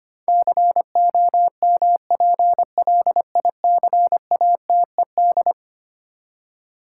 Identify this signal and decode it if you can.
Morse code, heard as a keyed tone, about 25 wpm, 700 Hz. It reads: COMPLICATED
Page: morse-code-ninja-practice-25wpm